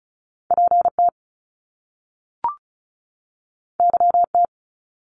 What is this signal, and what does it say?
Morse code, heard as a keyed tone, about 34 wpm, 700 Hz. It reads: PT YT
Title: Top Two Letter Combinations in Words 35wpm